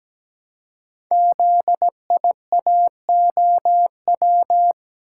Morse code, heard as a keyed tone, about 17 wpm, 700 Hz. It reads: ZIAOW
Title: Code Groups - 5 Letter:Numbers - 17wpm